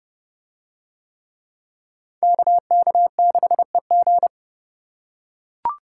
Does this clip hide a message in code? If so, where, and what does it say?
Yes: Morse code from 2.2 to 4.3 s, heard as a keyed tone, about 29 wpm, 700 Hz. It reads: KK6EG